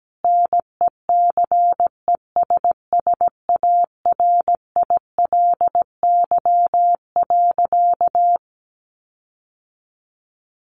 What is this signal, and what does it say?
Morse code, heard as a keyed tone, about 17 wpm, 700 Hz. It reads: NECESSARILY.
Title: Sentences from Top 1400 Words 17wpm